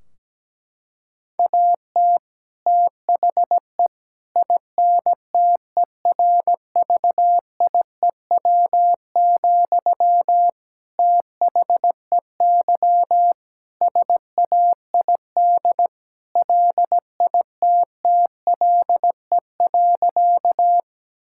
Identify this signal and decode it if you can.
Morse code, heard as a keyed tone, about 17 wpm, 700 Hz. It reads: AT THE INTERVIEW, THEY SAID LITTLE.
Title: Sentences from Top 1600 Words 17wpm